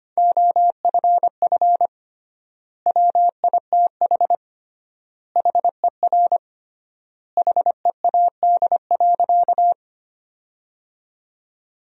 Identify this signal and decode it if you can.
Morse code, heard as a keyed tone, about 25 wpm, 700 Hz. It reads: OFF WITH HER HEAD.
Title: 3x Word-Spacing - Sentences from Top 300 Words - 25wpm